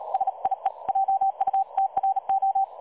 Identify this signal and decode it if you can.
Morse code, heard as a keyed tone, about 37 wpm, 780 Hz. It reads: SIE JUTRO
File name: dlazaawansowanych / nagraniazpasma / websdr recording 3565.0kHz